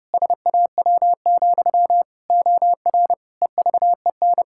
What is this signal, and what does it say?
Morse code, heard as a keyed tone, about 30 wpm, 700 Hz. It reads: SAW, OR EVEN
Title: Pangram Challenge v2 - 30wpm